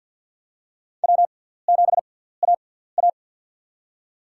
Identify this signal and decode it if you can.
Morse code, heard as a keyed tone, about 46 wpm, spaced as for 17 wpm, 700 Hz. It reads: W7AA
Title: USA Call Signs - 6x Character Spacing - 50wpm